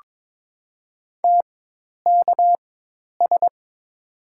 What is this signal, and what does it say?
Morse code, heard as a keyed tone, about 22 wpm, spaced as for 10 wpm, 700 Hz. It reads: TKS
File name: Top 90 QSO Elements - 4x Character Spacing - 22wpm